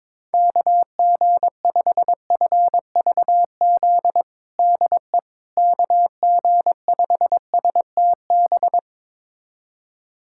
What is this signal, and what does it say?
Morse code, heard as a keyed tone, about 22 wpm, 700 Hz. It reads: KG5FVZ DE KG5STB
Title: USA Callsign DE Callsign - 6 Characters - 22wpm